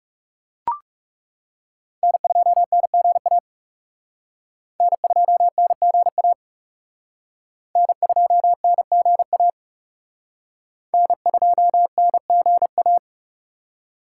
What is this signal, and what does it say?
Morse code, heard as a keyed tone, about 38 wpm, 700 Hz. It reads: N2NGA N2NGA N2NGA N2NGA